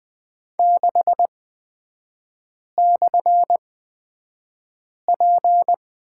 Morse code, heard as a keyed tone, about 20 wpm, spaced as for 5 wpm, 690 Hz. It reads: 6/P
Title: 229 Lesson 37e - Letter x - Three Characters 20wpm